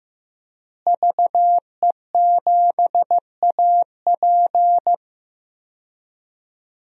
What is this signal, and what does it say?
Morse code, heard as a keyed tone, about 16 wpm, 700 Hz. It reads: VE7AP
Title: DXCC-100 Call Signs - 5 Characters - Repeated 3x - 15wpm